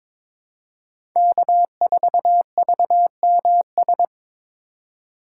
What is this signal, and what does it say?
Morse code, heard as a keyed tone, about 21 wpm, 700 Hz. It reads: K4VMS
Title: USA Call Signs - 5 Characters - 4x Character Spacing - 22wpm